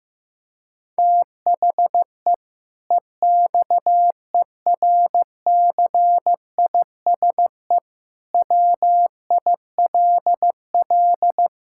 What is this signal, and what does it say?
Morse code, heard as a keyed tone, about 15 wpm, 700 Hz. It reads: THE EXERCISE WILL
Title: Sentences from Top 1000 Words - Review Part 6 - 15wpm